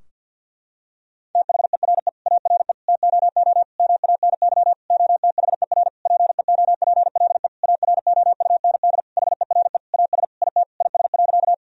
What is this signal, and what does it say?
Morse code, heard as a keyed tone, about 50 wpm, 700 Hz. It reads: THERE ARE TOO MANY OTHER PEOPLE AROUND HERE AS IT IS.
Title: Sentences from Top 1000 Words - Review Part 3 - 50wpm